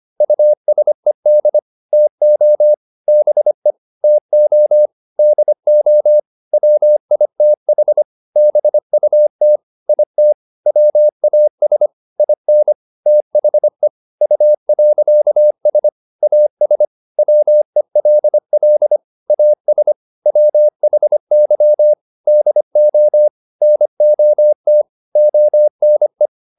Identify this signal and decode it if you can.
Morse code, heard as a keyed tone, about 25 wpm, 600 Hz. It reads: USED TO BE TO DO WITH BUT IT WAS IN THE U.S AS WELL AS WHY DO NOT ONE